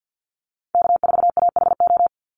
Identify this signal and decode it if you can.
Morse code, heard as a keyed tone, about 50 wpm, 700 Hz. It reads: K4AHO
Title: Call Signs - 50wpm